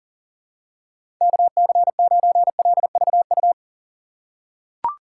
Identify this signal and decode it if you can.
Morse code, heard as a keyed tone, about 40 wpm, 700 Hz. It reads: KC9LVU
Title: USA Call Signs - 4x Character Spacing - 40wpm